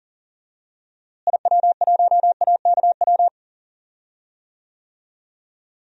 Morse code, heard as a keyed tone, about 37 wpm, 700 Hz. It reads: IW1AKW
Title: DXCC-100 Call Signs - 6 Characters - Repeated 3x - 40wpm